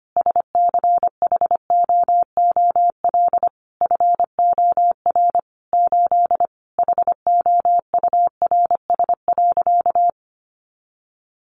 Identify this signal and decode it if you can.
Morse code, heard as a keyed tone, about 25 wpm, 700 Hz. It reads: SCHOOL FOR 8 HOURS.
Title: Additional Easy Phrases 25wpm